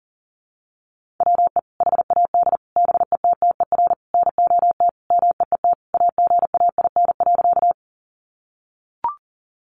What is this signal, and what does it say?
Morse code, heard as a keyed tone, about 40 wpm, 700 Hz. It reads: WE HAD BETTER NOT MEET AGAIN.
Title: Sentences from Top 1100 Words 40wpm